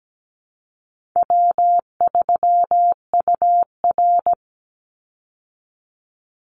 Morse code, heard as a keyed tone, about 17 wpm, 700 Hz. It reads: W3UR
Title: Easy Call Signs Repeated 2x - 17wpm